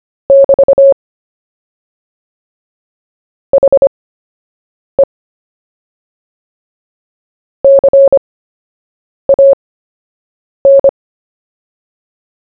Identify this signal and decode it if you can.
Morse code, heard as a keyed tone, about 25 wpm, spaced as for 7 wpm, 550 Hz. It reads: <BT> HE CAN
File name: cw / fists / Short-Story-1-7wpm